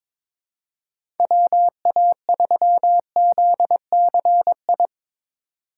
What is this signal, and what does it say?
Morse code, heard as a keyed tone, about 22 wpm, 700 Hz. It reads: WA3ZCI
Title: USA Call Signs - 6 Characters - 3x Character Spacing - 22wpm